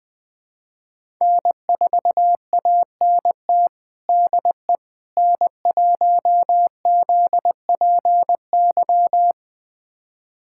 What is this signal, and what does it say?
Morse code, heard as a keyed tone, about 20 wpm, 700 Hz. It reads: N4ANT DE N1ZPY
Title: USA Callsign DE Callsign - 5 Characters - 20wpm